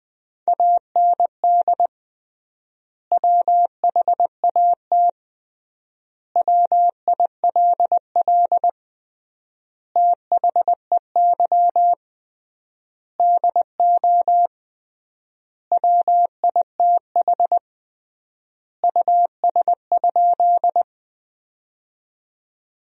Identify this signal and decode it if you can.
Morse code, heard as a keyed tone, about 20 wpm, 700 Hz. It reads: AND WHAT WILL THEY DO WITH US?
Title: 3x Word-Spacing - Sentences from Top 200 Words - 20wpm